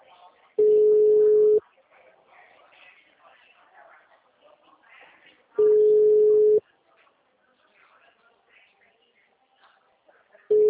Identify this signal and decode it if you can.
Ringback tone (Europe, 425 Hz), with speech.